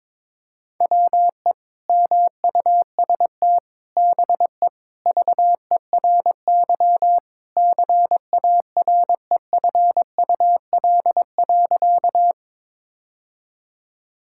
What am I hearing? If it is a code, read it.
Morse code, heard as a keyed tone, about 22 wpm, 700 Hz. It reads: WE MUST BE VERY CAREFUL.
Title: Sentences from Top 1600 Words 22wpm